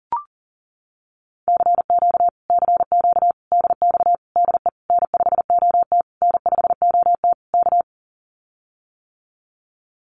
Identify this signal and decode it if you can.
Morse code, heard as a keyed tone, about 40 wpm, 700 Hz. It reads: CQ CQ DX DE N5OT N5OT K
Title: Common QSO Phrases Encore 40wpm